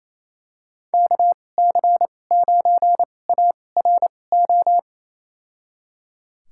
Morse code, heard as a keyed tone, about 28 wpm, spaced as for 20 wpm, 700 Hz. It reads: KC9ARO
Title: USA Call Signs - 2x Character Spacing - 28wpm